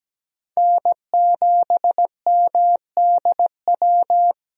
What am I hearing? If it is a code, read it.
Morse code, heard as a keyed tone, about 17 wpm, 700 Hz. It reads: N7MDW
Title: USA Call Signs - 5 Characters - Repeated 2x - 17wpm